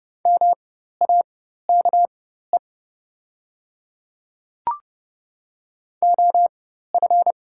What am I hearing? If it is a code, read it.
Morse code, heard as a keyed tone, about 30 wpm, spaced as for 14 wpm, 700 Hz. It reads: MAKE OF
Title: 4x Character Spacing - Top 100 Words - 30wpm